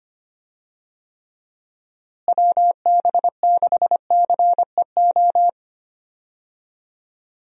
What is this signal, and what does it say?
Morse code, heard as a keyed tone, about 25 wpm, 700 Hz. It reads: WB6CEO
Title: USA Call Signs - 6 Characters - Repeated 3x - 25wpm